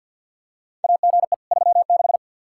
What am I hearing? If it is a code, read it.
Morse code, heard as a keyed tone, about 46 wpm, 700 Hz. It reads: AGE 36
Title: Practice QSO - Exchange 3 - More Info - 50wpm